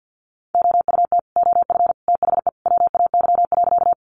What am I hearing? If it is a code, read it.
Morse code, heard as a keyed tone, about 50 wpm, 700 Hz. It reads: OUT OF THE WAY.
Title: Sentences from Top 100 Words 50wpm